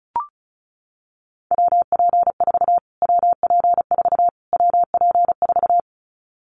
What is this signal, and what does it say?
Morse code, heard as a keyed tone, about 34 wpm, 700 Hz. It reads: WP4 WP4 WP4
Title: Callsign Prefixes for Top 25 DXCC - Repeated 3x - 35wpm